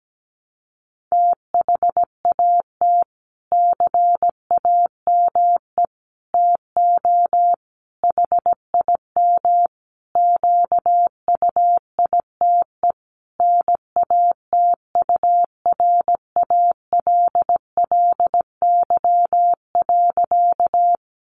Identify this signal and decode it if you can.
Morse code, heard as a keyed tone, about 17 wpm, 700 Hz. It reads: THAT CAME TO HIM QUITE NATURALLY.
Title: Sentences from Top 1300 Words 17wpm